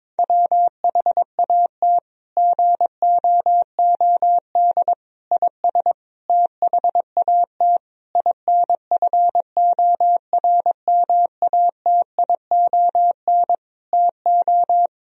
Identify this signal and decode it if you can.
Morse code, heard as a keyed tone, about 22 wpm, 700 Hz. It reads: WHAT GOOD IS THAT INFORMATION TO
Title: Sentences from Top 1000 Words - Review Part 3 - 22wpm